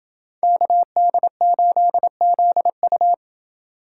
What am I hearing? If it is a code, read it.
Morse code, heard as a keyed tone, about 29 wpm, 700 Hz. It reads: KD8ZU